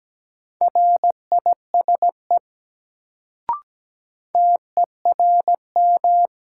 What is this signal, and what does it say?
Morse code, heard as a keyed tone, about 17 wpm, 700 Hz. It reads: RISE TERM
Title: Sentences from Top 900 Words - New Word Drill - 17wpm